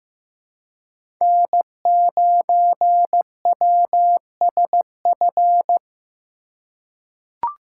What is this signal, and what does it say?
Morse code, heard as a keyed tone, about 15 wpm, 700 Hz. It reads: N9WSF
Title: USA Call Signs - 5 Characters - 3x Character Spacing - 15wpm